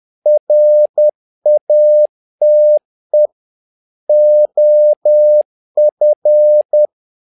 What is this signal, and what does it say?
Morse code, heard as a keyed tone, about 10 wpm, 600 Hz. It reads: RATE OF
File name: learning center / tower trouble / tower trouble 10wpm